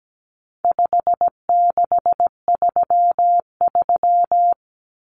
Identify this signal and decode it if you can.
Morse code, heard as a keyed tone, about 17 wpm, 700 Hz. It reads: 5633
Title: Numbers - 4-Digit 17wpm